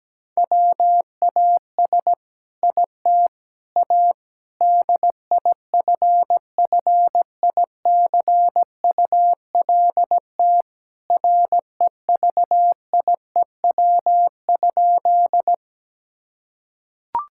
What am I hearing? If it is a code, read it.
Morse code, heard as a keyed tone, about 17 wpm, 700 Hz. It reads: WAS IT A DIFFICULT REVIEW?
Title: Sentences from Top 1000 Words - Review Part 6 - 17wpm